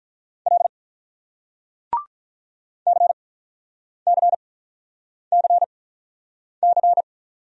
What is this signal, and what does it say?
Morse code, heard as a keyed tone, about 49 wpm, spaced as for 9 wpm, 700 Hz. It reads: L CCCC